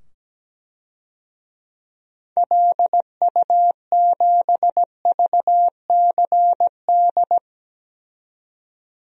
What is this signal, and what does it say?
Morse code, heard as a keyed tone, about 17 wpm, 700 Hz. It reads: LU7VCD
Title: DXCC-100 Call Signs - 6 Characters - 17wpm